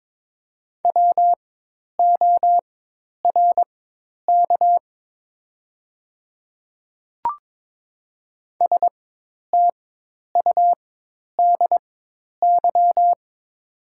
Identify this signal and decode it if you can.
Morse code, heard as a keyed tone, about 22 wpm, spaced as for 10 wpm, 700 Hz. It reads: WORK STUDY
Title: 4x Character Spacing - Top 300 Words - 22wpm